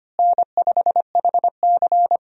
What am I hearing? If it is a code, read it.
Morse code, heard as a keyed tone, about 25 wpm, 700 Hz. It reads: N5HC
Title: USA Call Signs - 4 Characters - Repeated 3x - 25wpm